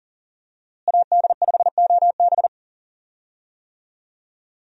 Morse code, heard as a keyed tone, about 40 wpm, 700 Hz. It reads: AD5OB
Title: USA Call Signs - 5 Characters - Repeated 2x - 40wpm